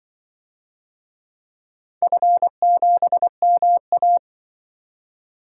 Morse code, heard as a keyed tone, about 24 wpm, 700 Hz. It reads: F7MA
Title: Speed-Racing - Four Letter-Number Combinations - 20wpm